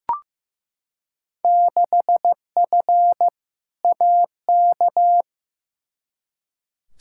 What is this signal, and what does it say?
Morse code, heard as a keyed tone, about 15 wpm, 700 Hz. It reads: 6F AK
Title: Field Day Exchanges Encore - 15wpm